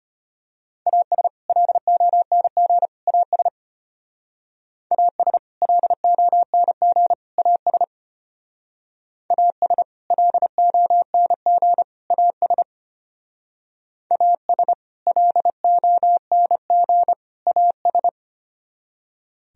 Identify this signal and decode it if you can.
Morse code, heard as a keyed tone, about 32 wpm, 700 Hz. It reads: AS LONG AS AS LONG AS AS LONG AS AS LONG AS